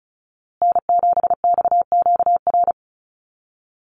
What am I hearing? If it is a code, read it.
Morse code, heard as a keyed tone, about 34 wpm, 700 Hz. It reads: N7XQR
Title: USA Call Signs Repeated 2x 35wpm